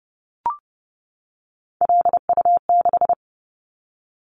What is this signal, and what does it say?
Morse code, heard as a keyed tone, about 30 wpm, 700 Hz. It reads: LU6
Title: Callsign Prefixes for Top 100 DXCC - 30wpm